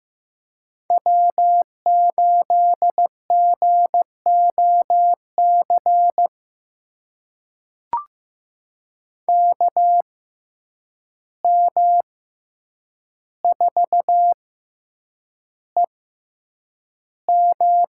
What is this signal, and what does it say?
Morse code, heard as a keyed tone, about 15 wpm, spaced as for 8 wpm, 700 Hz. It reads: W8GOC KM4EM
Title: USA Call Signs - 5 Characters - 6x Character Spacing - 15wpm